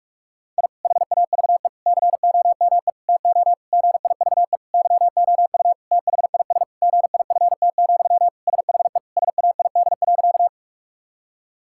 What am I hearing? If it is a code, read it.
Morse code, heard as a keyed tone, about 41 wpm, 700 Hz. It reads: I HAVE COME TO GIVE YOU THIS GIFT, SHE SAID.
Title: Sentences from Top 1000 Words - Review Part 6 - 45wpm